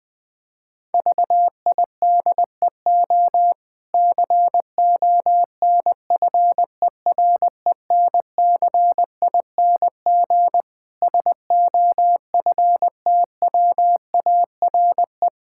Morse code, heard as a keyed tone, about 20 wpm, 700 Hz. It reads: VIDEO CONFERENCING SOFTWARE
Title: Difficult Words - Speed-Racing with Hints - 20wpm